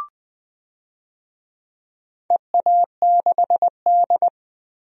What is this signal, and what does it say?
Morse code, heard as a keyed tone, about 20 wpm, 700 Hz. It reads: EA6D
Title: DXCC-100 Call Signs - 4 Characters - Repeated 3x - 20wpm